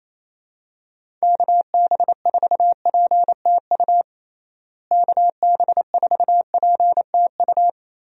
Morse code, heard as a keyed tone, about 28 wpm, 700 Hz. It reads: KB4PTU KB4PTU
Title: USA Call Signs - 6 Characters - Repeated 2x - 28wpm